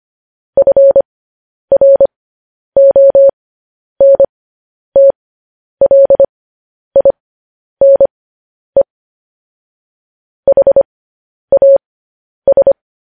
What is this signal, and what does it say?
Morse code, heard as a keyed tone, about 25 wpm, spaced as for 10 wpm, 550 Hz. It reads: FRONTLINE HAS
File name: podcast / News headlines-1x-25x10